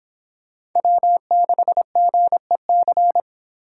Morse code, heard as a keyed tone, about 26 wpm, 700 Hz. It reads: W6GEC